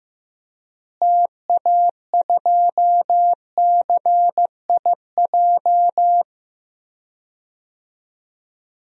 Morse code, heard as a keyed tone, about 16 wpm, 700 Hz. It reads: TA2CIJ
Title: DXCC-100 Call Signs - 6 Characters - Repeated 3x - 15wpm